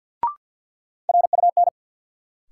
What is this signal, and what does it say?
Morse code, heard as a keyed tone, about 53 wpm, 700 Hz. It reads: RUN
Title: Sentences from Top 300 Words - New Word Drill - 50wpm